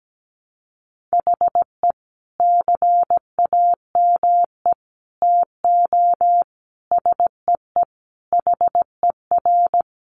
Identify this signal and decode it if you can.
Morse code, heard as a keyed tone, about 17 wpm, 700 Hz. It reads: HE CAME TO SEE HER
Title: Sentences from Top 1300 Words 17wpm